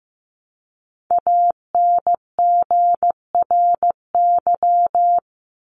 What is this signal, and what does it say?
Morse code, heard as a keyed tone, about 15 wpm, 700 Hz. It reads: ANGRY
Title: Top 1751-2000 Words 15wpm